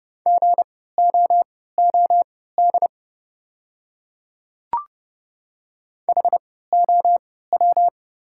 Morse code, heard as a keyed tone, about 32 wpm, spaced as for 17 wpm, 690 Hz. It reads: GOOD HOW
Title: 3x Character Spacing - Top 500 Words - 30wpm